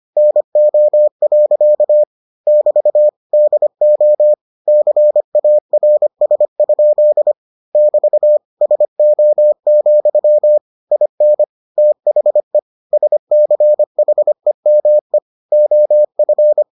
Morse code, heard as a keyed tone, about 25 wpm, 600 Hz. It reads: NO. <BT> DO CARS? <BT> SO, IN THE SCHEME OF